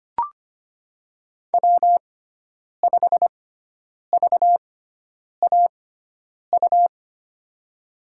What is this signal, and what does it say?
Morse code, heard as a keyed tone, about 25 wpm, spaced as for 9 wpm, 700 Hz. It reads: W5VAU
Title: USA Call Signs - 6x Character Spacing - 25wpm